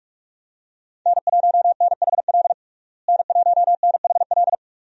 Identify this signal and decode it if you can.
Morse code, heard as a keyed tone, about 42 wpm, 700 Hz. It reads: N1NHL N1NHL
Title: USA Call Signs - 5 Characters - Repeated 2x - 45wpm